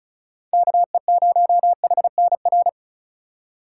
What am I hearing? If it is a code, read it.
Morse code, heard as a keyed tone, about 36 wpm, 700 Hz. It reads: KE0HNR
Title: USA Call Signs - 2x Character Spacing - 35wpm